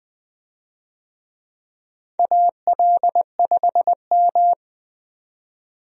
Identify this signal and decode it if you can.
Morse code, heard as a keyed tone, about 20 wpm, 700 Hz. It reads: AL5M
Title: USA Call Signs - 4 Characters - 20wpm